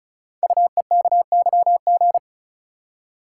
Morse code, heard as a keyed tone, about 36 wpm, 700 Hz. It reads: UEKYG
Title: Code Groups - 5 Letters - 35wpm